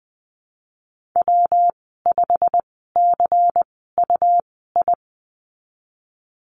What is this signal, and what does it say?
Morse code, heard as a keyed tone, about 20 wpm, spaced as for 15 wpm, 700 Hz. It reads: W5CUI
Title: 122 Lesson 20g - Number 3 - Call Signs 20wpm